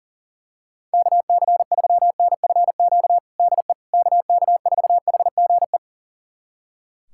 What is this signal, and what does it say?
Morse code, heard as a keyed tone, about 37 wpm, 700 Hz. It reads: KC3NFQ DE KK4HGE
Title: USA Callsign DE Callsign - 6 Characters - 40wpm